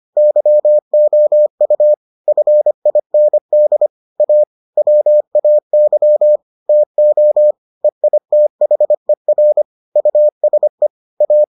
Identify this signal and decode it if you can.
Morse code, heard as a keyed tone, about 25 wpm, 600 Hz. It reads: YOU FIND A WAY TO EITHER USE A